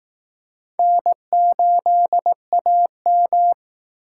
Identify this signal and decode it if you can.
Morse code, heard as a keyed tone, about 19 wpm, 700 Hz. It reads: N8AM
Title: Speed Racing - USA 1x2 Call Signs 15wpm